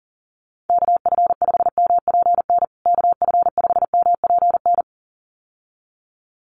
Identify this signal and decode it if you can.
Morse code, heard as a keyed tone, about 40 wpm, 700 Hz. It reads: KF5MPN KF5MPN
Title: USA Call Signs Repeated 2x 40wpm